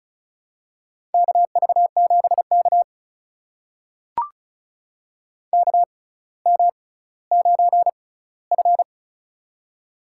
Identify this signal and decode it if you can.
Morse code, heard as a keyed tone, about 36 wpm, spaced as for 18 wpm, 700 Hz. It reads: KV7K KM9F
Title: USA Call Signs - 4 Characters - 6x Character Spacing - 35wpm